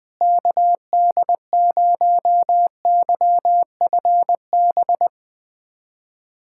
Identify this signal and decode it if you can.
Morse code, heard as a keyed tone, about 20 wpm, 700 Hz. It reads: KD0YFB
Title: USA Callsign DE Callsign - 6 Characters - 20wpm